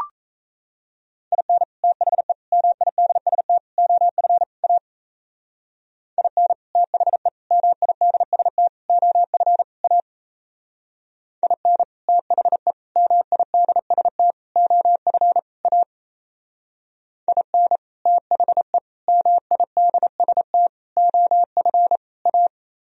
Morse code, heard as a keyed tone, about 34 wpm, 700 Hz. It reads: IN THE MIDST OF A IN THE MIDST OF A IN THE MIDST OF A IN THE MIDST OF A
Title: Speed-Racing - Sets of 5 Words 28wpm